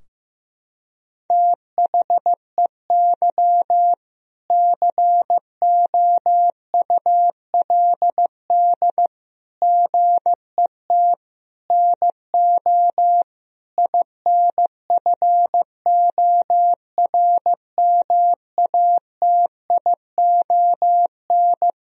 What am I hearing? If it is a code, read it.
Morse code, heard as a keyed tone, about 15 wpm, 700 Hz. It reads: THEY COULD GET NO INFORMATION
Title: Sentences from Top 1000 Words - Review Part 3 - 15wpm